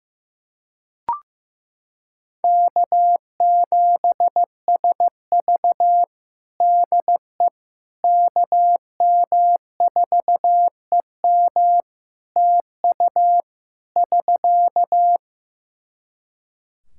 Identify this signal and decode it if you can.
Morse code, heard as a keyed tone, about 15 wpm, 700 Hz. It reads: K7SV DE KM4EM TU <SK>
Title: Practice QSO - Exchange 4 - Ending - 15wpm